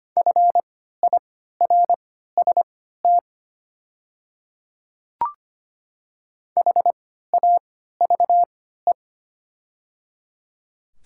Morse code, heard as a keyed tone, about 25 wpm, spaced as for 14 wpm, 700 Hz. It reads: FIRST HAVE
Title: 3x Character Spacing - Top 200 Words - 25wpm